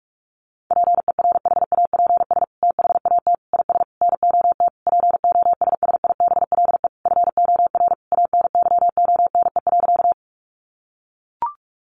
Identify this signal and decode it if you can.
Morse code, heard as a keyed tone, about 44 wpm, 700 Hz. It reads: PERHAPS THAT IS NOT POSSIBLE FOR ANYONE.
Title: Sentences from Top 900 Words 45wpm